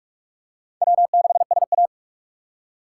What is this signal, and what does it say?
Morse code, heard as a keyed tone, about 39 wpm, 700 Hz. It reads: W6SA